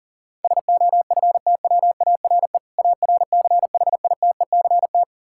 Morse code, heard as a keyed tone, about 40 wpm, 700 Hz. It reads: SOFTWARE ARCHITECT
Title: Practice QSO - Exchange 3 - More Info - 40wpm